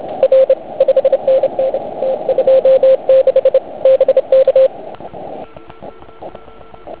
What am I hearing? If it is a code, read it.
Morse code, heard as a keyed tone, about 27 wpm, 530 Hz. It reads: R 5NN T26 BK
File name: Archiv / CUC149 / OK1IF CUC149